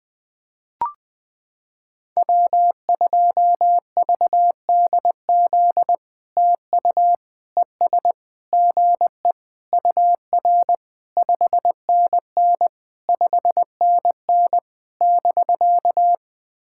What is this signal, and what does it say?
Morse code, heard as a keyed tone, about 20 wpm, 700 Hz. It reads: W2VDZ TU ES GE UR 5NN 5NN <BK>
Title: POTA QSO Practice - 20wpm